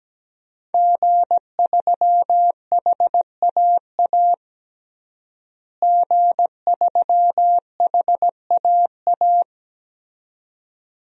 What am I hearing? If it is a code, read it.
Morse code, heard as a keyed tone, about 17 wpm, 700 Hz. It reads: G3HAA G3HAA
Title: DXCC-100 Call Signs - 5 Characters - Repeated 2x - 17wpm